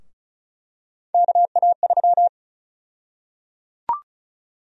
Morse code, heard as a keyed tone, about 36 wpm, 700 Hz. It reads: KA3
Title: USA Call Sign Prefixes - Repeated 2x - 35wpm